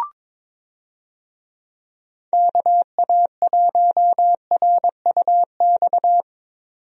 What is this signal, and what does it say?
Morse code, heard as a keyed tone, about 22 wpm, 700 Hz. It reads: KA1RUX
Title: USA Call Signs - 6 Characters - 22wpm